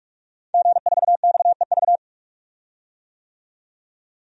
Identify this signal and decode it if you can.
Morse code, heard as a keyed tone, about 41 wpm, 700 Hz. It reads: G4XEV
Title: DXCC-100 Call Signs - 5 Characters - 45wpm